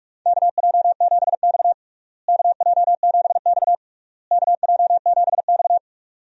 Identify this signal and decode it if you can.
Morse code, heard as a keyed tone, about 46 wpm, 700 Hz. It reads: KJ7X KJ7X KJ7X
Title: USA Call Signs - 4 Characters - Repeated 3x - 45wpm